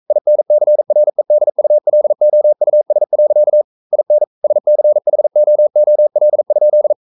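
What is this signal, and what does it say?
Morse code, heard as a keyed tone, about 42 wpm, 600 Hz. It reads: INCREDULOUS. IN SCHOOL?
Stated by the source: Morse code sending of an English novel at 42 wpm.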